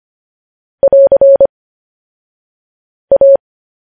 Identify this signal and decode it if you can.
Morse code, heard as a keyed tone, about 25 wpm, 550 Hz. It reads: <AR> A
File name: podcast / News headlines-1x-25x10